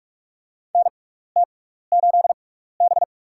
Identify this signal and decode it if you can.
Morse code, heard as a keyed tone, about 48 wpm, spaced as for 16 wpm, 700 Hz. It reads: NT8B
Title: USA Call Signs - 4 Characters - 6x Character Spacing - 45wpm